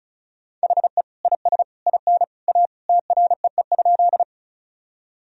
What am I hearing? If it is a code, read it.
Morse code, heard as a keyed tone, about 34 wpm, 700 Hz. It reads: HE IS IN A TREE?
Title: Sentences from Top 1000 Words - Review Part 1 - 35wpm